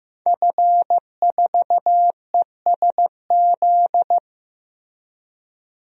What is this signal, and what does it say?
Morse code, heard as a keyed tone, about 15 wpm, 700 Hz. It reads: F4ESZ
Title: DXCC-100 Call Signs - 5 Characters - Repeated 2x - 15wpm